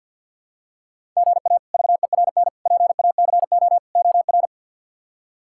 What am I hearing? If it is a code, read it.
Morse code, heard as a keyed tone, about 53 wpm, 700 Hz. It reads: GA VERN PACO OR
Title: K1USN SST QSO Practice - 50wpm